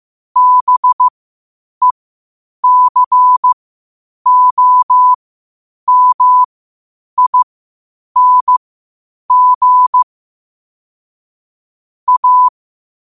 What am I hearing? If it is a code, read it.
Morse code, heard as a keyed tone, about 15 wpm, spaced as for 8 wpm, 1000 Hz. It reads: BECOMING A
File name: qotdfiles / 8wpm